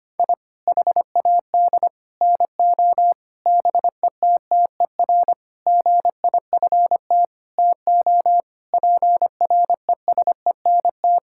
Morse code, heard as a keyed tone, about 25 wpm, 700 Hz. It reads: I HAD NO BETTER GIFT TO PRESENT
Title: Sentences from Top 1000 Words - Review Part 6 - 25wpm